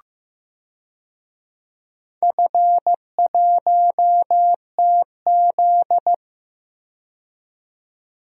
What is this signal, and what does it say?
Morse code, heard as a keyed tone, about 15 wpm, 700 Hz. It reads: F1TZ
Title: DXCC-100 Call Signs - 4 Characters - 15wpm